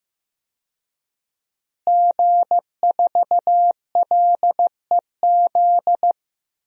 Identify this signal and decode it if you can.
Morse code, heard as a keyed tone, about 15 wpm, 700 Hz. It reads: G4LEZ
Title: DXCC-100 Call Signs - 5 Characters - Repeated 2x - 15wpm